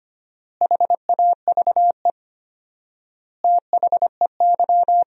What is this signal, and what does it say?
Morse code, heard as a keyed tone, about 25 wpm, 700 Hz. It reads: HAVE THEY
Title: 4x Word-Spacing - Sentences from Top 100 Words - 25wpm